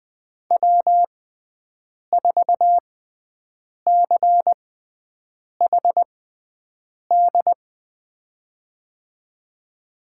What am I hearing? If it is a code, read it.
Morse code, heard as a keyed tone, about 20 wpm, spaced as for 7 wpm, 700 Hz. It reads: W4CHD
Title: USA Call Signs - 5 Characters - 6x Character Spacing - 20wpm